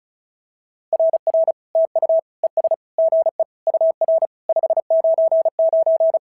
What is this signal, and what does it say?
Morse code, heard as a keyed tone, about 34 wpm, 650 Hz. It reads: RR TU ES GE UR 599
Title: SOTA QSO Practice 35wpm